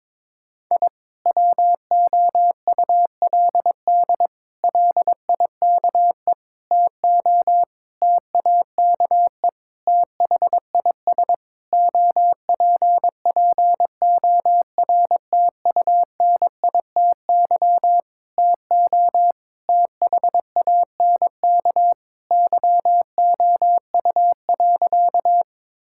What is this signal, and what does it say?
Morse code, heard as a keyed tone, about 22 wpm, 700 Hz. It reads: I WOULD LIKE TO TAKE THIS OPPORTUNITY TO THANK YOU.
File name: practice / 22wpm / Sentences with 8-Word Phrases - 22wpm